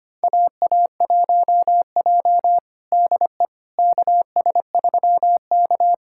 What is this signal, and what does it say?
Morse code, heard as a keyed tone, about 25 wpm, 700 Hz. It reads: AA1J DE KS3K
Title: USA Callsign DE Callsign - 4 Characters - 25wpm